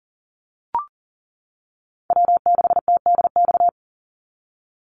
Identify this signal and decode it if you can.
Morse code, heard as a keyed tone, about 40 wpm, 700 Hz. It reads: W6TDX
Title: NA SOTA Chaser Callsigns 40wpm